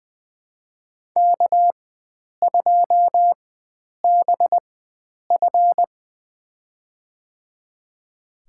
Morse code, heard as a keyed tone, about 20 wpm, spaced as for 9 wpm, 700 Hz. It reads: K2BF
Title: USA Call Signs - 4 Characters - 4x Character Spacing - 20wpm